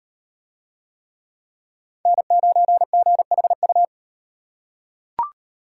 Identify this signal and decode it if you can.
Morse code, heard as a keyed tone, about 38 wpm, 700 Hz. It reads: N9GHU